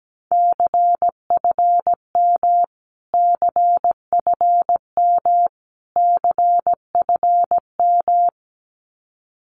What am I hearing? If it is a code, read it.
Morse code, heard as a keyed tone, about 17 wpm, 700 Hz. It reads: CFM CFM CFM
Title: Abbreviations Repeated 3x - 17wpm